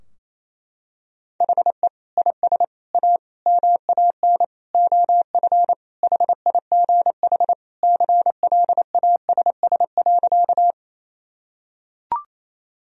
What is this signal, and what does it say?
Morse code, heard as a keyed tone, about 27 wpm, 700 Hz. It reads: HE IS A MAN OF HIGH CLASS.
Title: Sentences from Top 1000 Words - Review Part 2 - 28wpm